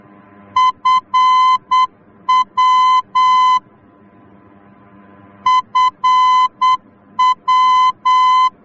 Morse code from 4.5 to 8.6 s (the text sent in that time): FW